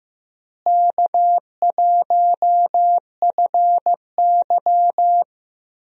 Morse code, heard as a keyed tone, about 15 wpm, 700 Hz. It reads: K1FY
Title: USA Call Signs - 4 Characters - Repeated 2x - 15wpm